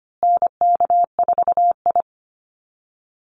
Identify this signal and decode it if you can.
Morse code, heard as a keyed tone, about 25 wpm, 700 Hz. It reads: NK4I
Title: Call Signs Repeated 2x - 25wpm